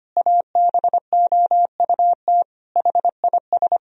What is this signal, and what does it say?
Morse code, heard as a keyed tone, about 25 wpm, 700 Hz. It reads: ABOUT HIS
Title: Sentences from Top 1000 Words - Review Part 4 - 25wpm